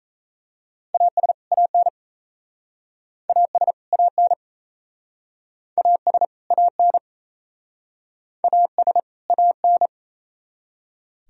Morse code, heard as a keyed tone, about 39 wpm, 700 Hz. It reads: AS AN AS AN AS AN AS AN